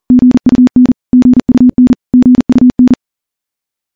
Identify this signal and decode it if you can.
Morse code, heard as a keyed tone, about 40 wpm, 250 Hz. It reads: GUN GUN GUN
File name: morse / morse-training-40WPM